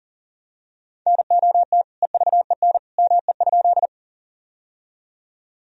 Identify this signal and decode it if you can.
Morse code, heard as a keyed tone, about 37 wpm, 700 Hz. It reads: NOT EVEN ME?
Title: Sentences from Top 1000 Words - Review Part 1 - 40wpm